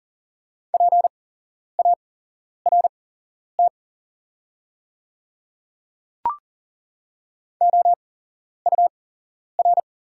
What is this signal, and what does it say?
Morse code, heard as a keyed tone, about 40 wpm, spaced as for 11 wpm, 700 Hz. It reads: PART OUR
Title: 8x Character Spacing - Top 200 Words - 40wpm